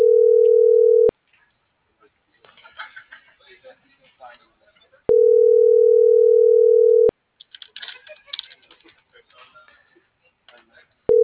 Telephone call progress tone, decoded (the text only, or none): ringback tone